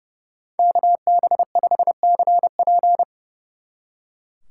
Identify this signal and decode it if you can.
Morse code, heard as a keyed tone, about 36 wpm, 700 Hz. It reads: KB5CP
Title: Speed Racing - USA 2x2 Call Signs 30wpm